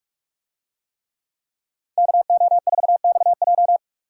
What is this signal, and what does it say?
Morse code, heard as a keyed tone, about 42 wpm, 700 Hz. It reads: KO4XJ